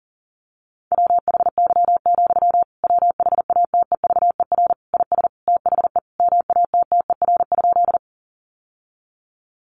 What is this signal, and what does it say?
Morse code, heard as a keyed tone, about 40 wpm, 700 Hz. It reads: WHY, WHATEVER IS THE MATTER?
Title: Sentences from Top 1100 Words 40wpm